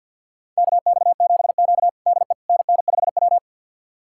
Morse code, heard as a keyed tone, about 49 wpm, 700 Hz. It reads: KX7Q DE NN5W
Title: USA Callsign DE Callsign - 4 Characters - 50wpm